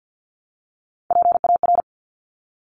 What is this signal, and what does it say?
Morse code, heard as a keyed tone, about 50 wpm, 700 Hz. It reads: PAR
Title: Top Three Letter Combinations in Words 50wpm